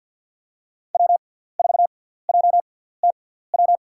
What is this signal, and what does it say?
Morse code, heard as a keyed tone, about 52 wpm, spaced as for 17 wpm, 700 Hz. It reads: W4JTW
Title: USA Call Signs - 5 Characters - 6x Character Spacing - 50wpm